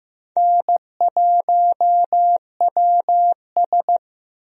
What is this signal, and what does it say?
Morse code, heard as a keyed tone, about 15 wpm, 700 Hz. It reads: N1WS
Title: USA Call Signs - 4 Characters - Repeated 3x - 15wpm